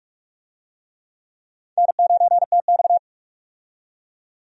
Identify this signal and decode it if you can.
Morse code, heard as a keyed tone, about 46 wpm, 700 Hz. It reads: N9TX
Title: USA Call Signs - 4 Characters - 45wpm